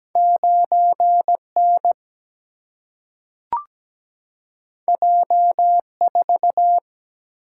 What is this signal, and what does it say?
Morse code, heard as a keyed tone, about 17 wpm, 700 Hz. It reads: 9N J4
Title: Single-Letter Single-Number 17wpm